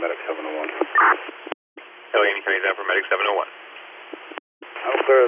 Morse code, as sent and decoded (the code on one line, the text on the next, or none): none